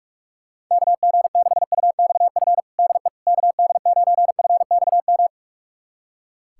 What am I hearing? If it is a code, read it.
Morse code, heard as a keyed tone, about 46 wpm, 700 Hz. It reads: KG6UXF DE KD9FXM
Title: USA Callsign DE Callsign 45wpm